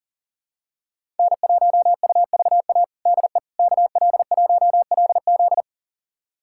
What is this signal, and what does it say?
Morse code, heard as a keyed tone, about 37 wpm, 700 Hz. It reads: N1UVA DE KL1LZ
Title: USA Callsign DE Callsign - 5 Characters - 40wpm